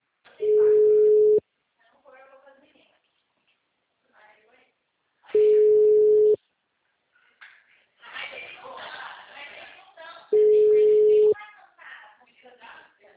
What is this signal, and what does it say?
Ringback tone (Europe, 425 Hz), with speech.